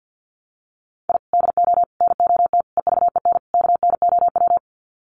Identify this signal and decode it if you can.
Morse code, heard as a keyed tone, about 48 wpm, 700 Hz. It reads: I DO NOT EVEN KNOW
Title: Sets of 5 Words 50wpm